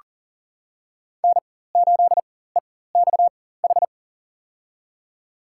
Morse code, heard as a keyed tone, about 40 wpm, spaced as for 19 wpm, 700 Hz. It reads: N8EXH
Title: USA Call Signs - 4x Character Spacing - 40wpm